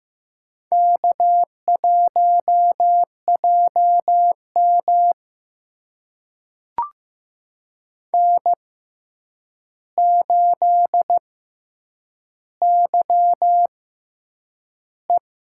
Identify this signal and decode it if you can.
Morse code, heard as a keyed tone, about 15 wpm, spaced as for 8 wpm, 700 Hz. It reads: K1JM N8YE
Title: USA Call Signs - 4 Characters - 6x Character Spacing - 15wpm